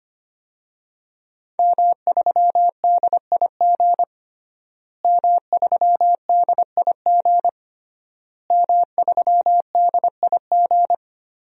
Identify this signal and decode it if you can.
Morse code, heard as a keyed tone, about 25 wpm, 700 Hz. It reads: M3DIG M3DIG M3DIG
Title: DXCC-100 Call Signs - 5 Characters - Repeated 3x - 25wpm